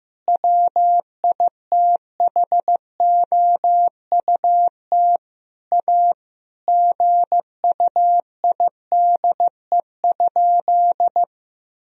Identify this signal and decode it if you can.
Morse code, heard as a keyed tone, about 15 wpm, 700 Hz. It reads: WITHOUT A GUIDE?
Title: Sentences from Top 1000 Words - Review Part 3 - 15wpm